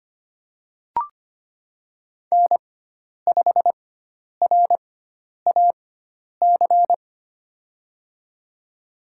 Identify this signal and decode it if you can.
Morse code, heard as a keyed tone, about 25 wpm, spaced as for 10 wpm, 700 Hz. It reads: N5RAC
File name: Lesson 16g - Letter c - Call Signs 25wpm